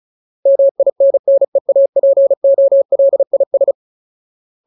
Morse code, heard as a keyed tone, about 36 wpm, 550 Hz. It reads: MINNEAPOLIS
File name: Ultimate Word List / 35WPM Ultimate